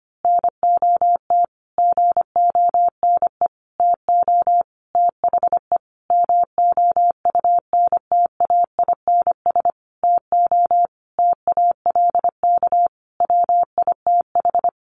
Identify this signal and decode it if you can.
Morse code, heard as a keyed tone, about 25 wpm, 700 Hz. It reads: NOT GONE TO THE MOUNTAINS TO TALK WITH
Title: A Sound Of Thunder 25wpm